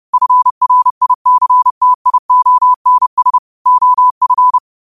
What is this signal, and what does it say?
Morse code, heard as a keyed tone, about 30 wpm, 1000 Hz. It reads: FRICTIONS OF